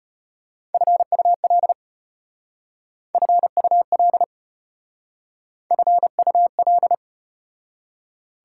Morse code, heard as a keyed tone, about 34 wpm, 700 Hz. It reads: FUL FUL FUL